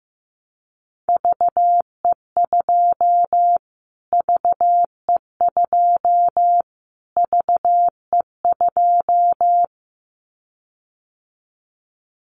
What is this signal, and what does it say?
Morse code, heard as a keyed tone, about 15 wpm, 700 Hz. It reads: VE2 VE2 VE2
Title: Callsign Prefixes for Top 25 DXCC - Repeated 3x - 15wpm